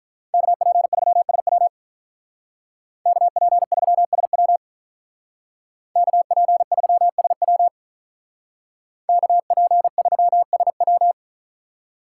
Morse code, heard as a keyed tone, about 43 wpm, 700 Hz. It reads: KP3SW KP3SW KP3SW KP3SW
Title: Speed Racing - USA Call Signs 35wpm